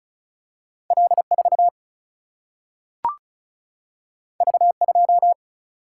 Morse code, heard as a keyed tone, about 36 wpm, 700 Hz. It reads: L4 V2
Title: Single-Letter Single-Number 35wpm